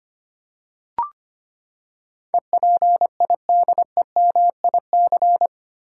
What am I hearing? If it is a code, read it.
Morse code, heard as a keyed tone, about 25 wpm, 700 Hz. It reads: EPIDEMIC
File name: Eight Letter Words - 25wpm